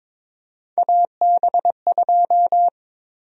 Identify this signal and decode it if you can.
Morse code, heard as a keyed tone, about 22 wpm, 700 Hz. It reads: AB2
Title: USA Call Sign Prefixes - 22wpm